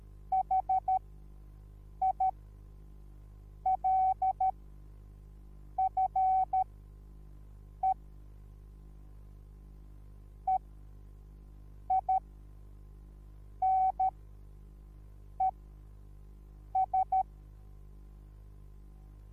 Morse code, heard as a keyed tone, about 13 wpm, spaced as for 5 wpm, 760 Hz. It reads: HILFE EINES
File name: Onlinelehrgang / morsen / DJ4UF Morsen21